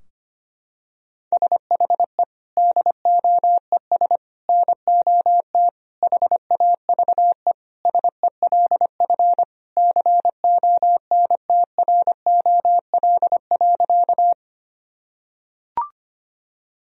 Morse code, heard as a keyed tone, about 25 wpm, 700 Hz. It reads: SHE DOES NOT HAVE SELF CONTROL.
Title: Sentences from Top 1000 Words - Review Part 3 - 25wpm